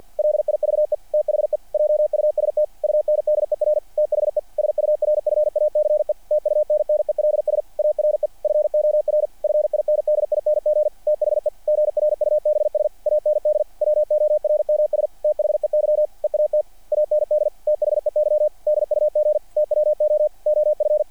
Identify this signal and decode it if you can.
Morse code, heard as a keyed tone, about 48 wpm, 600 Hz. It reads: LIVE THE JUST UNDER THE SURFACE TUNNELS ARE FOR FINDING THE GRUBS AND WORMS THEY EAT AND THEY DUG TWO OF
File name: archived / CRSnet